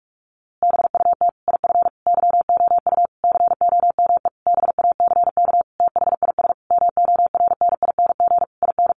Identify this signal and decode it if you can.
Morse code, heard as a keyed tone, about 44 wpm, 700 Hz. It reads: BUT IF YOU COME BACK THIS MORNING IN
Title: A Sound Of Thunder 45wpm